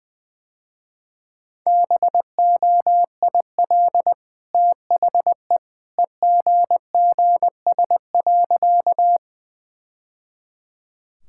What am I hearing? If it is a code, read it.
Morse code, heard as a keyed tone, about 21 wpm, 700 Hz. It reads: BOIL THE EGGS.
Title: Easy Sentences with Hints - 20wpm